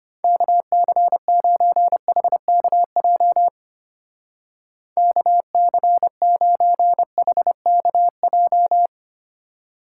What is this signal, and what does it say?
Morse code, heard as a keyed tone, about 31 wpm, 700 Hz. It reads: KC9HKJ KC9HKJ
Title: Speed Racing - USA Call Signs 25wpm